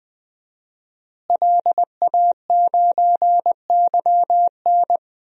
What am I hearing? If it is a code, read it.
Morse code, heard as a keyed tone, about 20 wpm, 700 Hz. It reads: LA9YN
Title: DXCC-100 Call Signs - 5 Characters - Repeated 2x - 20wpm